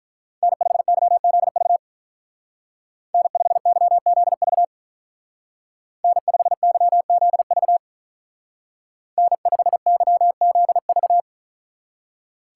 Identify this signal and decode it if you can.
Morse code, heard as a keyed tone, about 44 wpm, 700 Hz. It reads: N5YZV N5YZV N5YZV N5YZV
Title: Speed Racing - USA 1x3 Call Signs 35wpm